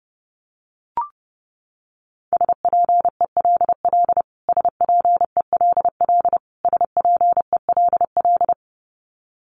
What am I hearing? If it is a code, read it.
Morse code, heard as a keyed tone, about 30 wpm, 700 Hz. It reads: SPELL SPELL SPELL
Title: Top 500 Words Repeated 30wpm